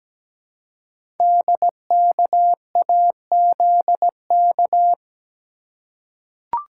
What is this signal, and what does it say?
Morse code, heard as a keyed tone, about 17 wpm, 700 Hz. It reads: DKAZK
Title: Code Groups - 5 Letters - 17wpm